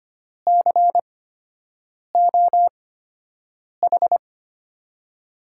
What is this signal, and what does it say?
Morse code, heard as a keyed tone, about 25 wpm, spaced as for 7 wpm, 700 Hz. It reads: COH
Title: USA Call Signs - 6 Characters - 8x Character Spacing - 25wpm